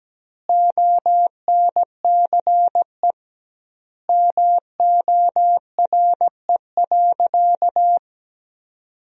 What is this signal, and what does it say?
Morse code, heard as a keyed tone, about 17 wpm, 700 Hz. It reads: ONCE MORE.
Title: 2x Word-Spacing - Sentences from Top 500 Words - 17wpm